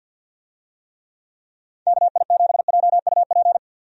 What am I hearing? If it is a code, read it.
Morse code, heard as a keyed tone, about 50 wpm, 700 Hz. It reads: KI7JUP
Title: USA Call Signs - 6 Characters - Repeated 2x - 50wpm